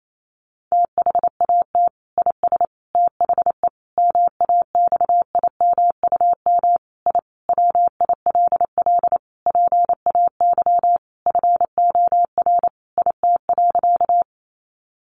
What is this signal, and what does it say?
Morse code, heard as a keyed tone, about 28 wpm, 700 Hz. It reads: THAT IS THE MAXIMUM I WILL PAY FOR IT.
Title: Sentences from Top 1300 Words 28wpm